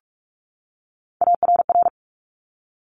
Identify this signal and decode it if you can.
Morse code, heard as a keyed tone, about 46 wpm, 690 Hz. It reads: ARR
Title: Top Three Letter Combinations in Words 45wpm